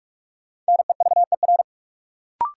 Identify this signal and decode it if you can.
Morse code, heard as a keyed tone, about 44 wpm, 700 Hz. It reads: NEVER
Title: Sentences from Top 300 Words - New Word Drill - 45wpm